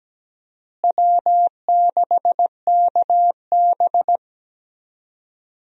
Morse code, heard as a keyed tone, about 17 wpm, 700 Hz. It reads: W6KB
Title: USA Call Signs - 4 Characters - Repeated 3x - 17wpm